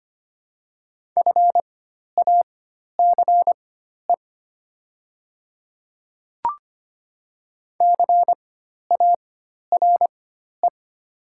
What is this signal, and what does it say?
Morse code, heard as a keyed tone, about 25 wpm, spaced as for 12 wpm, 700 Hz. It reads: FACE CARE
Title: 4x Character Spacing - Top 300 Words - 25wpm